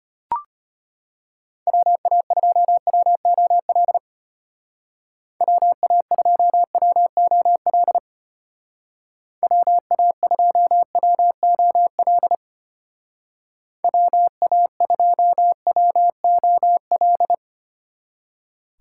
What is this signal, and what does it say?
Morse code, heard as a keyed tone, about 30 wpm, 700 Hz. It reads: WA2WOL WA2WOL WA2WOL WA2WOL
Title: Speed Racing - USA 2x3 Call Signs 25wpm